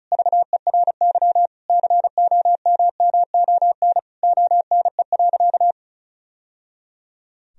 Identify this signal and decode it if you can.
Morse code, heard as a keyed tone, about 36 wpm, 700 Hz. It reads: VERY COMMON ONE.
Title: Sentences from Top 1000 Words - Review Part 4 - 35wpm